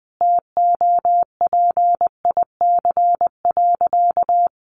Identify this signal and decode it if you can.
Morse code, heard as a keyed tone, about 20 wpm, 700 Hz. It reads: TOPIC.
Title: Sentences from Top 1200 Words 20wpm